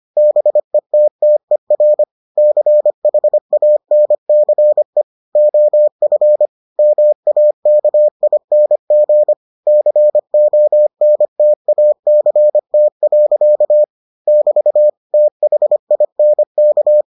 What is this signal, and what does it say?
Morse code, heard as a keyed tone, about 25 wpm, 600 Hz. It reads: BETTER CHANCE OF MAKING CONTACT. <BT> THINK